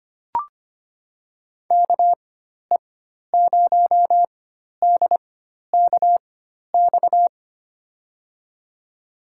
Morse code, heard as a keyed tone, about 25 wpm, spaced as for 12 wpm, 700 Hz. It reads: KE0DKX
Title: USA Call Signs - 6 Characters - 4x Character Spacing - 25wpm